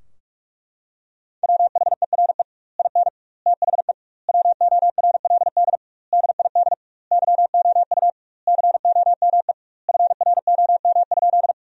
Morse code, heard as a keyed tone, about 41 wpm, 700 Hz. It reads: WHERE IN THE WORLD DID YOU COME FROM?
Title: 2x Word-Spacing - Sentences from Top 200 Words - 45wpm